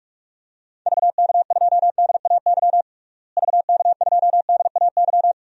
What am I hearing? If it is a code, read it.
Morse code, heard as a keyed tone, about 41 wpm, 700 Hz. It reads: VK2DAY VK2DAY
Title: DXCC-100 Call Signs - 6 Characters - Repeated 3x - 45wpm